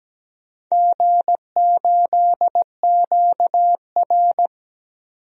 Morse code, heard as a keyed tone, about 17 wpm, 700 Hz. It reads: G8QR
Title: DXCC-100 Call Signs - 4 Characters - 17wpm